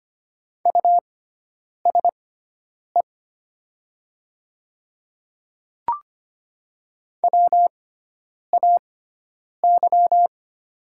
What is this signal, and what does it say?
Morse code, heard as a keyed tone, about 25 wpm, spaced as for 9 wpm, 700 Hz. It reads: USE WAY
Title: 6x Character Spacing - Top 100 Words - 25wpm